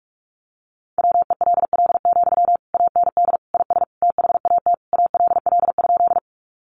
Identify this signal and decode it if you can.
Morse code, heard as a keyed tone, about 44 wpm, 700 Hz. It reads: WELL, AND IS THAT ALL?
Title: Sentences from Top 200 Words 45wpm